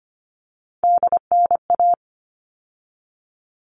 Morse code, heard as a keyed tone, about 25 wpm, 700 Hz. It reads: DNA
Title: All Cut Numbers - 3-Digit 25wpm